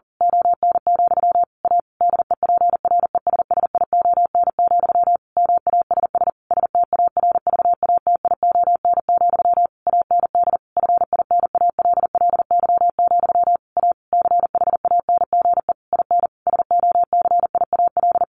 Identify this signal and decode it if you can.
Morse code, heard as a keyed tone, about 40 wpm, 700 Hz. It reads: ON, A DEPRESSION, MASS STARVATION, AND FINALLY, A CHANGE IN SOCIAL
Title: A Sound Of Thunder 40wpm